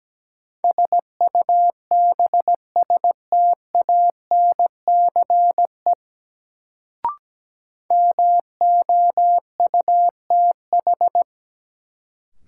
Morse code, heard as a keyed tone, about 17 wpm, 700 Hz. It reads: SUBSTANCE MOUTH
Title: Sentences from Top 1500 Words - New Word Drill Alt - 17wpm